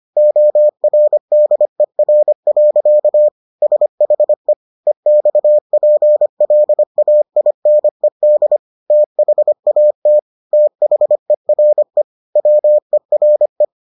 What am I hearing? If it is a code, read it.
Morse code, heard as a keyed tone, about 25 wpm, 600 Hz. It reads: ORDER. SHE EXPLAINED THAT THERE WERE